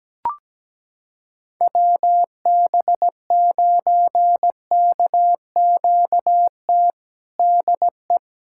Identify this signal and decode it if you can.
Morse code, heard as a keyed tone, about 17 wpm, 700 Hz. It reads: WB9KQT DE
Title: USA Callsign DE Callsign - 6 Characters - 17wpm